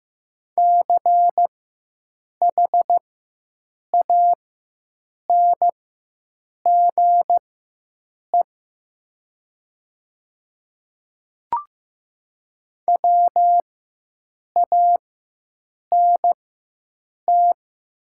Morse code, heard as a keyed tone, about 15 wpm, spaced as for 7 wpm, 700 Hz. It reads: CHANGE WANT
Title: 4x Character Spacing - Top 200 Words - 15wpm